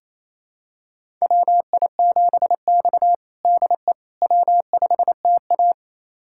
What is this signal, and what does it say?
Morse code, heard as a keyed tone, about 28 wpm, 700 Hz. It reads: WI7X DE W5TA
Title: USA Callsign DE Callsign - 4 Characters - 28wpm